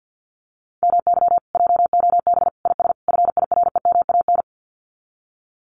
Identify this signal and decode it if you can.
Morse code, heard as a keyed tone, about 50 wpm, 700 Hz. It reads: MY JOB IS FIREMAN
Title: Common QSO Phrases Encore 50wpm